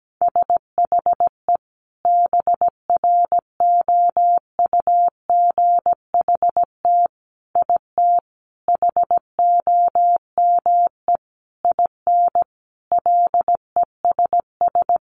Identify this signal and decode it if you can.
Morse code, heard as a keyed tone, about 17 wpm, 700 Hz. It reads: SHE BROUGHT IT HOME IN LESS
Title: Sentences from Top 500 Words 17wpm